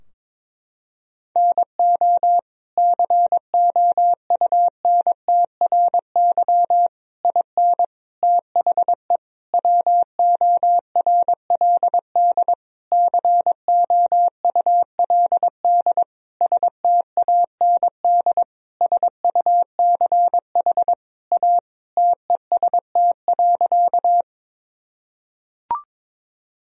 Morse code, heard as a keyed tone, about 22 wpm, 700 Hz. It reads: NO COUNTRY IN THE WORLD COULD STAND SUCH A TEST.
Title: Sentences from Top 1000 Words - Review Part 3 - 22wpm